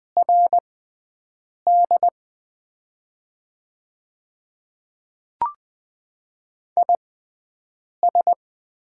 Morse code, heard as a keyed tone, about 20 wpm, spaced as for 7 wpm, 700 Hz. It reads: RD IS
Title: 6x Character Spacing - Top 500 Words - 20wpm